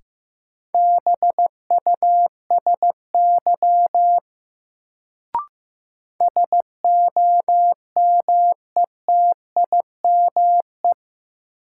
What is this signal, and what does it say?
Morse code, heard as a keyed tone, about 15 wpm, 700 Hz. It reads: BUSY SOMETIME
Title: Sentences from Top 1500 Words - New Word Drill Alt - 15wpm